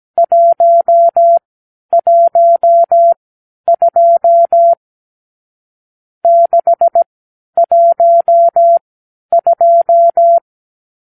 Morse code, heard as a keyed tone, about 17 wpm, spaced as for 10 wpm, 680 Hz. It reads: 112 612
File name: k7qo / Lesson 50 (1-6)